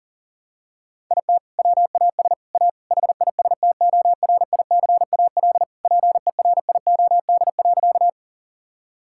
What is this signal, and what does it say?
Morse code, heard as a keyed tone, about 40 wpm, 700 Hz. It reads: IT WAS A HISTORICAL PERIOD.
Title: Sentences from Top 1000 Words - Review Part 5 - 40wpm